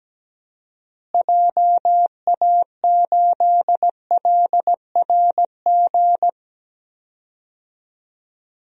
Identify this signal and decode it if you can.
Morse code, heard as a keyed tone, about 17 wpm, 700 Hz. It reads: JA8LRG
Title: DXCC-100 Call Signs - 6 Characters - Repeated 3x - 17wpm